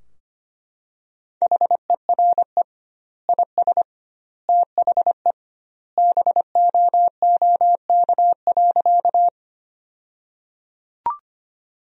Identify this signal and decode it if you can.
Morse code, heard as a keyed tone, about 25 wpm, 700 Hz. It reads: HERE IS THE BOOK.
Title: 2x Word-Spacing - Sentences from Top 300 Words - 25wpm